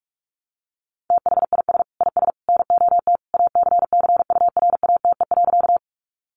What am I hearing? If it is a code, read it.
Morse code, heard as a keyed tone, about 44 wpm, 700 Hz. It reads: THIS IS NOT ACCURATE.
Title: Sentences from Top 1200 Words 45wpm